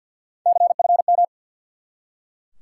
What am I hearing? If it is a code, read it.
Morse code, heard as a keyed tone, about 52 wpm, 700 Hz. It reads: CFM
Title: Top 90 QSO Elements 50wpm